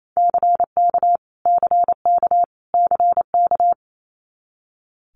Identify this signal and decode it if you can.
Morse code, heard as a keyed tone, about 28 wpm, 700 Hz. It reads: CK CK CK
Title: Multi-letter Phonemes Repeated 3x 28wpm